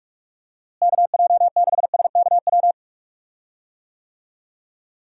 Morse code, heard as a keyed tone, about 43 wpm, 700 Hz. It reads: KJ6SKW
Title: USA Call Signs - 6 Characters - Repeated 3x - 45wpm